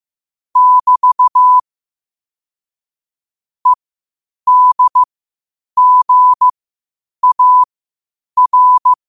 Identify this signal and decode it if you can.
Morse code, heard as a keyed tone, about 15 wpm, spaced as for 8 wpm, 1000 Hz. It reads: <BT> EDGAR
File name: qotdfiles / 8wpm